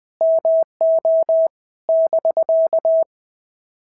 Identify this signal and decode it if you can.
Morse code, heard as a keyed tone, about 20 wpm, 650 Hz. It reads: MO <BK>
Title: POTA QSO Practice - 20wpm